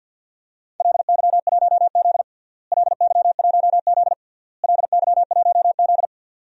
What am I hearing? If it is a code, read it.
Morse code, heard as a keyed tone, about 47 wpm, 700 Hz. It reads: LY1Z LY1Z LY1Z
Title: DXCC-100 Call Signs - 4 Characters - Repeated 3x - 50wpm